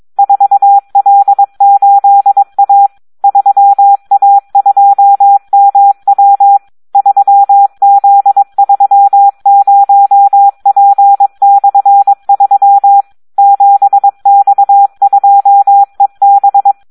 Morse code, heard as a keyed tone, about 22 wpm, 800 Hz. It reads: 4L8A 3A2MW 3Z30P/3 7X2EB
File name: Audio practica CW / SD 1 22